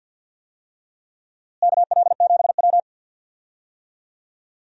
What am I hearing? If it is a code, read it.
Morse code, heard as a keyed tone, about 53 wpm, 700 Hz. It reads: KL7W
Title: DXCC-100 Call Signs - 4 Characters - Repeated 3x - 50wpm